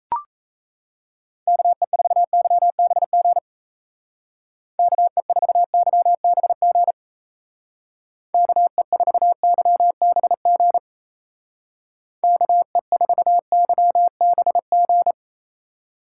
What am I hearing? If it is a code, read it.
Morse code, heard as a keyed tone, about 35 wpm, 700 Hz. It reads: KE4YBG KE4YBG KE4YBG KE4YBG
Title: Speed Racing - USA 2x3 Call Signs 28wpm